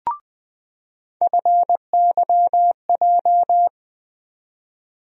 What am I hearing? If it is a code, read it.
Morse code, heard as a keyed tone, about 20 wpm, 700 Hz. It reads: FYJ
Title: USA Call Sign Postfixes - 3 Letters 20wpm